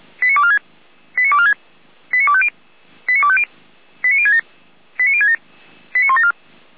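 Five-tone selective call (ZVEI), heard as quick runs of five tones, tones of about 70 ms. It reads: 89246 89246 89250 89250 89067 89067 89174
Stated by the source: ZVEI five-tone selective call received on SDR, from feuerwehr-orlamuende (Orlamünde fire brigade).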